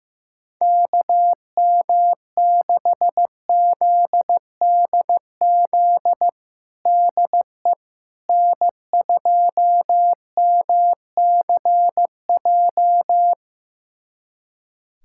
Morse code, heard as a keyed tone, about 15 wpm, 700 Hz. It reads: KM6ZDZ DE N2MCJ
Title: USA Callsign DE Callsign 15wpm